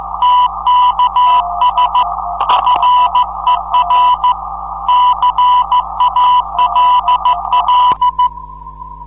Morse code, heard as a keyed tone, about 14 wpm, 990 Hz, with noise. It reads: TKS FER CALL
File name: 472khz qso web / 29march2014 / 01 dl4yhf 29mar2014 2248utc